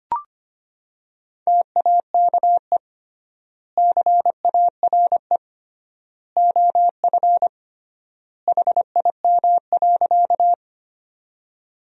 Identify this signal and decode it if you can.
Morse code, heard as a keyed tone, about 25 wpm, 700 Hz. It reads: TAKE CARE OF HIM.
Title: 3x Word-Spacing - Sentences from Top 300 Words - 25wpm